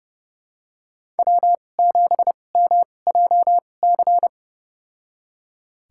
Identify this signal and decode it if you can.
Morse code, heard as a keyed tone, about 30 wpm, spaced as for 22 wpm, 700 Hz. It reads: W7MJC
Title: USA Call Signs - 2x Character Spacing - 30wpm